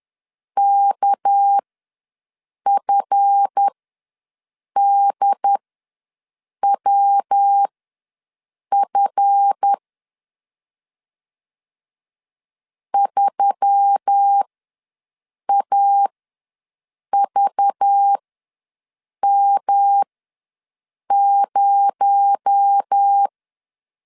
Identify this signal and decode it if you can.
Morse code, heard as a keyed tone, about 10 wpm, spaced as for 6 wpm, 800 Hz. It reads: KFDWF 3AVM0
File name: LEZIONI / 7 GF9Z / 7 GF ws 10wpm 3918